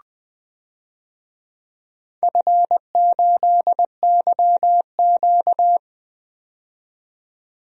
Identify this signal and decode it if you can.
Morse code, heard as a keyed tone, about 21 wpm, 700 Hz. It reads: F8YQ
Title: DXCC-100 Call Signs - 4 Characters - 20wpm